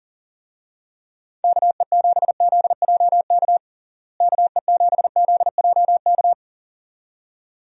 Morse code, heard as a keyed tone, about 40 wpm, 700 Hz. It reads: KE7ZJK KE7ZJK
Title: USA Call Signs - 6 Characters - Repeated 2x - 40wpm